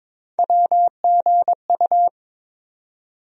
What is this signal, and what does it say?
Morse code, heard as a keyed tone, about 22 wpm, 700 Hz. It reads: WGU
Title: USA Call Sign Postfixes - 3 Letters 22wpm